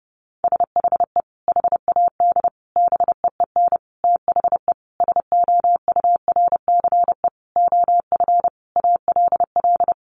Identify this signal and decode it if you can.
Morse code, heard as a keyed tone, about 30 wpm, 700 Hz. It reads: SHE HAD BEEN THE SOURCE OF ALL
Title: Sentences from Top 1200 Words 30wpm